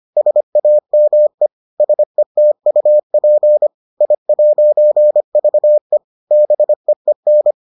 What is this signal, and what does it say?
Morse code, heard as a keyed tone, about 25 wpm, 600 Hz. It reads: SAME SETUP I'VE BEEN